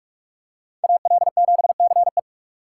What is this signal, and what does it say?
Morse code, heard as a keyed tone, about 42 wpm, 690 Hz. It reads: AL7CE
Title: USA Call Signs - 5 Characters - 2x Character Spacing - 45wpm